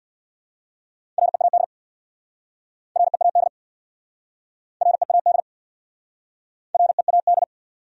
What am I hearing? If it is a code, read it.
Morse code, heard as a keyed tone, about 69 wpm, 690 Hz. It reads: READ READ READ READ